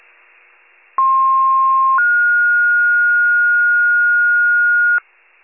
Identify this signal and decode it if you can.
Two-tone page, heard as a one-second tone, then a three-second tone, about 1060 then 1430 Hz, with noise. It is heard from 1.0 to 5.0 s.
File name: wcalert